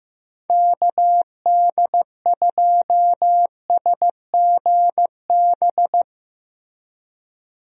Morse code, heard as a keyed tone, about 15 wpm, 700 Hz. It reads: KD2SGB
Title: USA Call Signs - 6 Characters - Repeated 2x - 15wpm